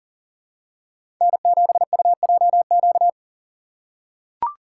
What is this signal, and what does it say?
Morse code, heard as a keyed tone, about 40 wpm, 700 Hz. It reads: N7UJQ
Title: USA Call Signs - 5 Characters - 4x Character Spacing - 40wpm